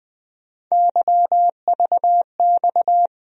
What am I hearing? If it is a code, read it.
Morse code, heard as a keyed tone, about 21 wpm, 700 Hz. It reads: YVX
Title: USA Call Sign Postfixes - 3 Letters 20wpm